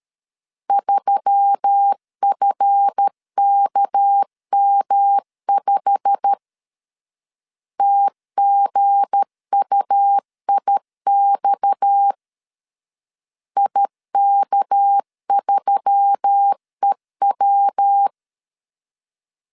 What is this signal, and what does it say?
Morse code, heard as a keyed tone, about 13 wpm, 800 Hz. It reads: 3FKM5 TGUIX IK3EW
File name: LEZIONI / 9 2 12wpm 2301